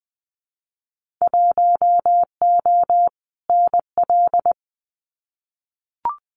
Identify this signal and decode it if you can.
Morse code, heard as a keyed tone, about 20 wpm, 700 Hz. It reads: 1O NL
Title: Winter Field Day Exchanges Encore 20wpm